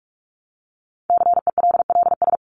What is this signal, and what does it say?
Morse code, heard as a keyed tone, about 44 wpm, 700 Hz. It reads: CELLS
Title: ES:S Suffix - Top 100 Words with Base Words 45wpm